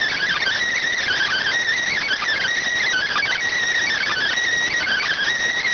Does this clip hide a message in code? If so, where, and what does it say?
Yes: Emergency Alert System (SAME) tones (header bursts) from 1.0 to 5.3 s, heard as harsh buzzing bursts.